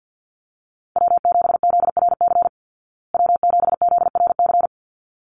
Morse code, heard as a keyed tone, about 48 wpm, 700 Hz. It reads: W7ZRC W7ZRC
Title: Call Signs Repeated 3x - 50wpm